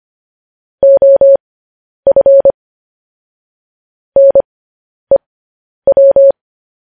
Morse code, heard as a keyed tone, about 25 wpm, spaced as for 10 wpm, 550 Hz. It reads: OF NEW